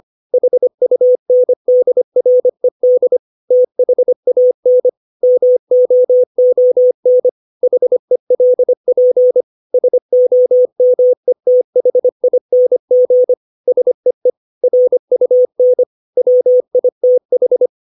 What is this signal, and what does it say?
Morse code, heard as a keyed tone, about 25 wpm, 500 Hz. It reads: HUNDRED THAN MOON HELP SOMETHING SEE RUN WITH
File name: 632 words / 632 Words 25WPM